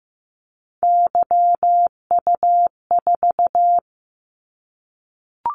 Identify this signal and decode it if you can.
Morse code, heard as a keyed tone, about 15 wpm, 700 Hz. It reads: YU4
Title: Callsign Prefixes for Top 25 DXCC - 15wpm